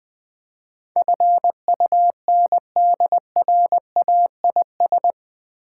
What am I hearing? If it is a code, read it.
Morse code, heard as a keyed tone, about 20 wpm, 700 Hz. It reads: FUNDRAIS
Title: Build A Word - 4 Letters At A Time - 11-Letter Words - 20wpm